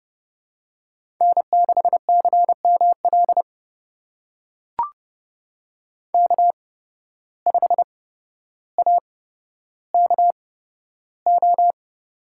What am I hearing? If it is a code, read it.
Morse code, heard as a keyed tone, about 30 wpm, spaced as for 13 wpm, 700 Hz. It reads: N6CML K5AKO
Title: USA Call Signs - 8x Character Spacing - 30wpm